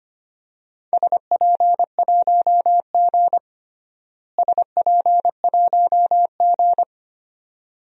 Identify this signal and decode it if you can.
Morse code, heard as a keyed tone, about 25 wpm, 700 Hz. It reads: SP1G SP1G
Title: DXCC-100 Call Signs - 4 Characters - Repeated 2x - 25wpm